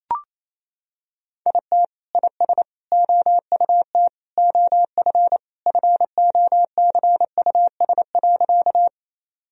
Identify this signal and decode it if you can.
Morse code, heard as a keyed tone, about 28 wpm, 700 Hz. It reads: IT IS OUT OF FOCUS.
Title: Sentences from Top 1000 Words - Review Part 4 - 28wpm